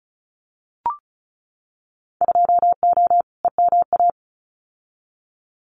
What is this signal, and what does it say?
Morse code, heard as a keyed tone, about 34 wpm, 700 Hz. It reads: 2O EMA
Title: Winter Field Day Exchanges 35wpm